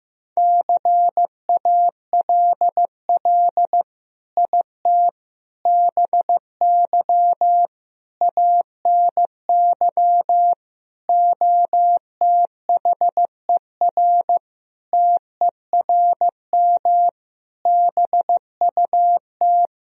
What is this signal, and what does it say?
Morse code, heard as a keyed tone, about 16 wpm, 700 Hz. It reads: CALL IT BY ANY OTHER TERM BUT
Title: Sentences from Top 1000 Words - Review Part 4 - 15wpm